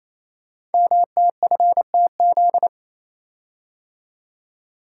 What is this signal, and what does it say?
Morse code, heard as a keyed tone, about 27 wpm, 700 Hz. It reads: MTFTZ
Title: Code Groups - 5 Letter:Numbers - 28wpm